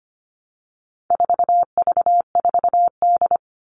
Morse code, heard as a keyed tone, about 25 wpm, 700 Hz. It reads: 4V4D
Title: All Cut Numbers - 4-Digit 25wpm